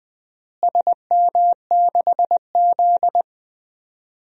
Morse code, heard as a keyed tone, about 20 wpm, 700 Hz. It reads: SM6Z
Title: DXCC-100 Call Signs - 4 Characters - Repeated 2x - 20wpm